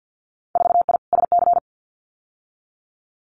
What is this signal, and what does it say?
Morse code, heard as a keyed tone, about 50 wpm, 700 Hz. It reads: 4I SC
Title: Winter Field Day Exchanges 50wpm